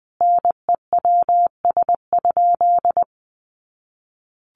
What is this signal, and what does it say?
Morse code, heard as a keyed tone, about 20 wpm, 700 Hz. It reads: NEWS?
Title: Sentences from Top 800 Words 20wpm